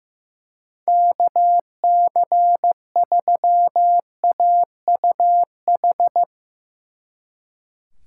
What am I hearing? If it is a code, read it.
Morse code, heard as a keyed tone, about 15 wpm, 700 Hz. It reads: KC3AUH
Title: Speed Racing - USA 2x3 Call Signs 15wpm